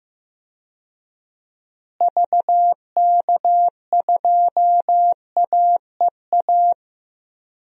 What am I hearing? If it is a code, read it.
Morse code, heard as a keyed tone, about 16 wpm, 700 Hz. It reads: VK2AEA
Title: DXCC-100 Call Signs - 6 Characters - Repeated 2x - 15wpm